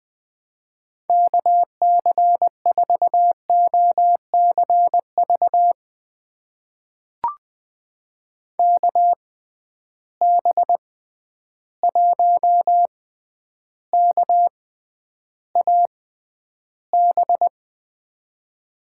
Morse code, heard as a keyed tone, about 20 wpm, spaced as for 10 wpm, 700 Hz. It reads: KC4OCV KB1KAB
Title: USA Call Signs - 6 Characters - 6x Character Spacing - 20wpm